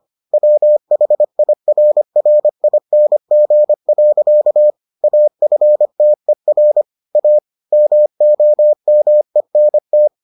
Morse code, heard as a keyed tone, about 25 wpm, 600 Hz. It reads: WHIRRING. AFTER A MOMENT